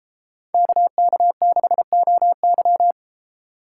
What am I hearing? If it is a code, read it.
Morse code, heard as a keyed tone, about 29 wpm, 700 Hz. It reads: KK6OY